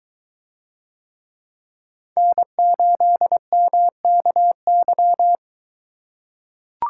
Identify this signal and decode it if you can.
Morse code, heard as a keyed tone, about 22 wpm, 700 Hz. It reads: N8MKY